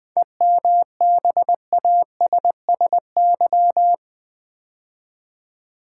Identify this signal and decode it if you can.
Morse code, heard as a keyed tone, about 20 wpm, 700 Hz. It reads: EMBASSY
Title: 7 Letter Words - First 3 Letters Given - 20wpm